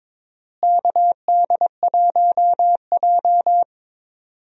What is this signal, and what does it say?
Morse code, heard as a keyed tone, about 22 wpm, 700 Hz. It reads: KD1J
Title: USA Call Signs - 4 Characters - Repeated 2x - 22wpm